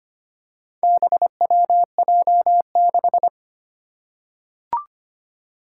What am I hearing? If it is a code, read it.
Morse code, heard as a keyed tone, about 25 wpm, 700 Hz. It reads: BWJ6
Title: Four Challenging Letter-Number Combinations 25wpm